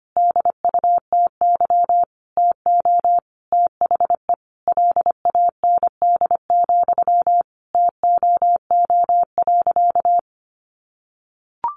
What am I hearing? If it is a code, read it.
Morse code, heard as a keyed tone, about 25 wpm, 700 Hz. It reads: DUTY TO THE LAND, TOO.
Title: Sentences from Top 1500 Words 25wpm